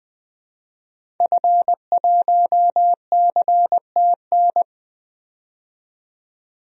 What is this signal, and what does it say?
Morse code, heard as a keyed tone, about 20 wpm, 700 Hz. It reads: F1CTN
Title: DXCC-100 Call Signs - 5 Characters - 20wpm